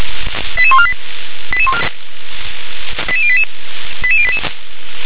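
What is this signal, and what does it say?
Five-tone selective call (ZVEI), heard as quick runs of five tones, tones of about 70 ms. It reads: 88148 88088 88088
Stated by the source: ZVEI five-tone selective call received on an SDR.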